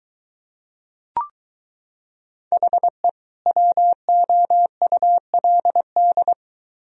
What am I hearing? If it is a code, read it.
Morse code, heard as a keyed tone, about 23 wpm, 700 Hz. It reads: HE WOULD